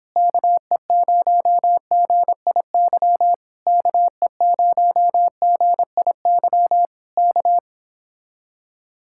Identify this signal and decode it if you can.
Morse code, heard as a keyed tone, about 26 wpm, 700 Hz. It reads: KE0GIY KE0GIY K
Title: Speed-Racing Common QSO Phrases Encore - 22wpm